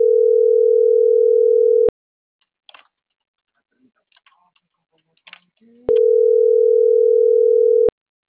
Ringback tone (North America), 440 + 480 Hz.